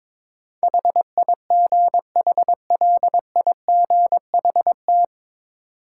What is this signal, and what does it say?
Morse code, heard as a keyed tone, about 21 wpm, 700 Hz. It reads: HIGHLIGHT
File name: 9 Letter Words - First 3 Letters Given - 22wpm